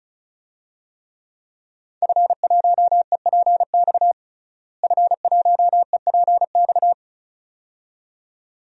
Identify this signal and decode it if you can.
Morse code, heard as a keyed tone, about 36 wpm, 700 Hz. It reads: F1EPX F1EPX
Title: DXCC-100 Call Signs - 5 Characters - Repeated 2x - 35wpm